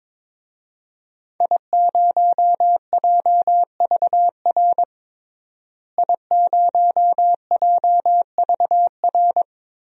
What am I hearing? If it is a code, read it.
Morse code, heard as a keyed tone, about 22 wpm, 700 Hz. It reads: I0JVR I0JVR
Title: DXCC-100 Call Signs - 5 Characters - Repeated 3x - 22wpm